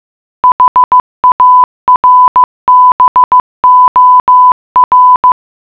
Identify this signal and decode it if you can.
Morse code, heard as a keyed tone, about 15 wpm, 1000 Hz. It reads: HARBOR